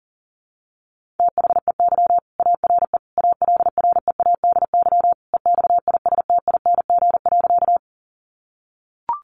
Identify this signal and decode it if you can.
Morse code, heard as a keyed tone, about 40 wpm, 700 Hz. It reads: THEY ARE ALREADY EXISTING.
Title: Sentences from Top 1500 Words 40wpm